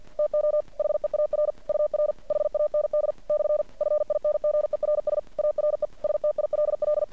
Morse code, heard as a keyed tone, about 49 wpm, 600 Hz. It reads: TO HEAR UR HAND / FINGERS ARE STILL
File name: archived / CRSnet